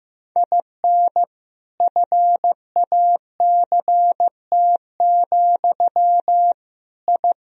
Morse code, heard as a keyed tone, about 15 wpm, 700 Hz. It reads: IN FACT, I
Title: Sentences from Top 1000 Words - Review Part 3 - 15wpm